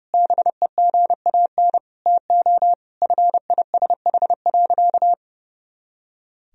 Morse code, heard as a keyed tone, about 30 wpm, 700 Hz. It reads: BEGAN TO FISH.
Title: Sentences from Top 300 Words and Abbreviations 30wpm